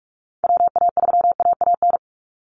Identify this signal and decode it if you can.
Morse code, heard as a keyed tone, about 44 wpm, 700 Hz. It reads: WA3AAN
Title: Call Signs Repeated 3x - 45wpm